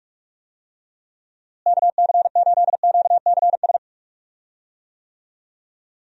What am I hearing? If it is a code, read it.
Morse code, heard as a keyed tone, about 46 wpm, 700 Hz. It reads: KC8QCS
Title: USA Call Signs - 6 Characters - Repeated 3x - 45wpm